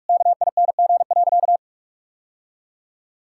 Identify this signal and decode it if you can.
Morse code, heard as a keyed tone, about 48 wpm, 700 Hz. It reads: KING.
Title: Sentences from Top 1000 Words - Review Part 6 - 45wpm